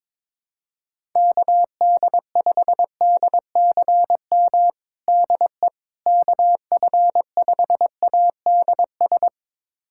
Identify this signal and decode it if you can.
Morse code, heard as a keyed tone, about 22 wpm, 700 Hz. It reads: KD5DCM DE KF5ADS
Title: USA Callsign DE Callsign 22wpm